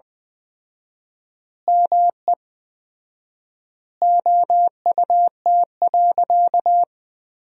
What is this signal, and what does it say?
Morse code, heard as a keyed tone, about 20 wpm, 700 Hz. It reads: ME OUT.
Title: 4x Word-Spacing - Sentences from Top 500 Words - 20wpm